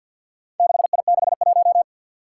Morse code, heard as a keyed tone, about 47 wpm, 700 Hz. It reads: 6I61
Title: Four Challenging Letter-Number Combinations 50wpm